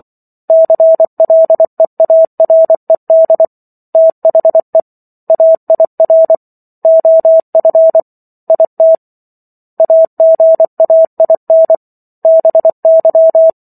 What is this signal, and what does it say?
Morse code, heard as a keyed tone, about 24 wpm, 650 Hz. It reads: CLEARED THE AIR OF IT AGAIN BY